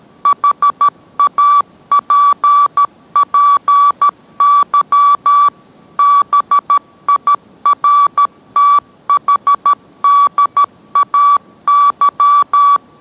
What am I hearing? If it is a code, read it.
Morse code, heard as a keyed tone, about 13 wpm, 1150 Hz. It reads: HAPPY BIRTHDAY